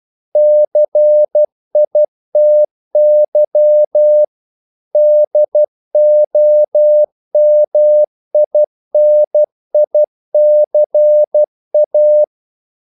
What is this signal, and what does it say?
Morse code, heard as a keyed tone, about 12 wpm, 600 Hz. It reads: CITY DOMINICA
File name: CW / Practice / NationalCapitals12